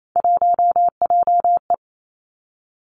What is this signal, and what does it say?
Morse code, heard as a keyed tone, about 28 wpm, 690 Hz. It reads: 1JE
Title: Three Challenging Letter-Number Combinations 28wpm